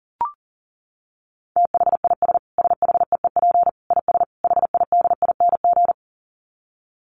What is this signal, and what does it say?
Morse code, heard as a keyed tone, about 40 wpm, 700 Hz. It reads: THIS SHEEP IS HIDING
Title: Dit Challenge 40wpm